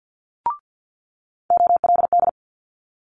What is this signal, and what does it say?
Morse code, heard as a keyed tone, about 50 wpm, 700 Hz. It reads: OLD
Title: Multi-letter Phonemes Quick Fire 50wpm